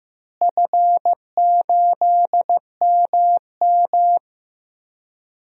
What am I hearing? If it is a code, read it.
Morse code, heard as a keyed tone, about 15 wpm, 700 Hz. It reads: F8MM
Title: DXCC-100 Call Signs - 4 Characters - Repeated 3x - 15wpm